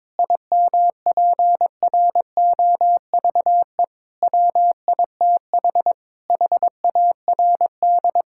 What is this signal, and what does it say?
Morse code, heard as a keyed tone, about 21 wpm, 700 Hz. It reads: IMPROVE WITH HARD
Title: Sentences from Top 1000 Words - Review Part 5 - 22wpm